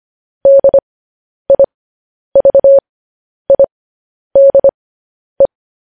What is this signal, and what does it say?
Morse code, heard as a keyed tone, about 25 wpm, spaced as for 10 wpm, 550 Hz. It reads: DIVIDE
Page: news headlines in morse code at 10 wpm